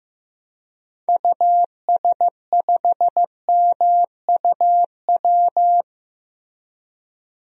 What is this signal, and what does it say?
Morse code, heard as a keyed tone, about 15 wpm, 700 Hz. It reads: US5MUW
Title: DXCC-100 Call Signs - 6 Characters - Repeated 3x - 15wpm